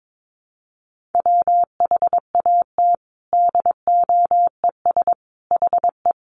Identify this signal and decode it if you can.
Morse code, heard as a keyed tone, about 22 wpm, 700 Hz. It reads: WHAT DOES HE
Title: Sentences from Top 1300 Words 22wpm